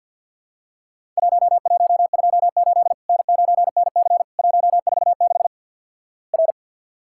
Morse code, heard as a keyed tone, about 50 wpm, 700 Hz. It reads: 1128 N9NC 146 R
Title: Worked All Europe DX Contest - QTC Exchange with Repeats - 50wpm